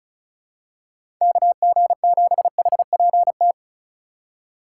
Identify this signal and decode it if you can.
Morse code, heard as a keyed tone, about 36 wpm, 700 Hz. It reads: KG7HPT
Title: USA Call Signs - 6 Characters - 2x Character Spacing - 35wpm